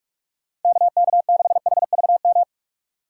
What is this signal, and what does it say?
Morse code, heard as a keyed tone, about 44 wpm, 700 Hz. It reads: KK6HVM
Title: USA Call Signs - 6 Characters - 6x Character Spacing - 45wpm